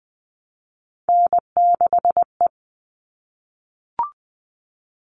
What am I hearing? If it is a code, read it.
Morse code, heard as a keyed tone, about 20 wpm, 700 Hz. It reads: N6E
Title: All Cut Numbers - 3-Digit 20wpm